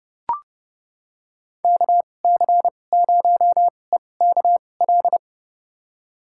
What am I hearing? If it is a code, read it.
Morse code, heard as a keyed tone, about 32 wpm, spaced as for 22 wpm, 700 Hz. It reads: KC0EKL
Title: USA Call Signs - 6 Characters - 2x Character Spacing - 30wpm